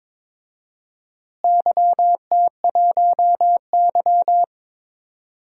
Morse code, heard as a keyed tone, about 22 wpm, 700 Hz. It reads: YT1Y
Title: DXCC-100 Call Signs - 4 Characters - Repeated 2x - 22wpm